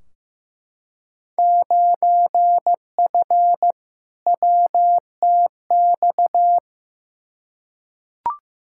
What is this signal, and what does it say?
Morse code, heard as a keyed tone, about 15 wpm, 700 Hz. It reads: 9F WTX
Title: Field Day Exchanges Encore - 15wpm